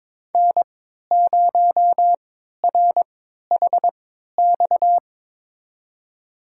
Morse code, heard as a keyed tone, about 22 wpm, spaced as for 13 wpm, 700 Hz. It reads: N0RHX
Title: USA Call Signs - 5 Characters - 3x Character Spacing - 22wpm